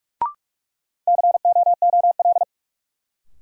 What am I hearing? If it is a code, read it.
Morse code, heard as a keyed tone, about 46 wpm, 700 Hz. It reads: COOL
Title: Sentences from Top 1200 Words - New Word Drill - 45wpm